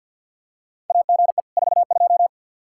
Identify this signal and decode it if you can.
Morse code, heard as a keyed tone, about 47 wpm, 700 Hz. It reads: AGE 42
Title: Practice QSO - Exchange 3 - More Info - 50wpm